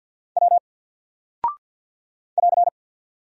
Morse code, heard as a keyed tone, about 50 wpm, 700 Hz. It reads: W <AR>
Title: Single Character (Letter:Number:Punctuation:Prosign) 50wpm